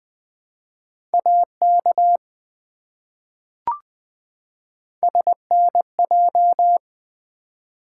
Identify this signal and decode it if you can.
Morse code, heard as a keyed tone, about 20 wpm, 700 Hz. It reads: AK SNJ
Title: ARRL Sections v2 - 20wpm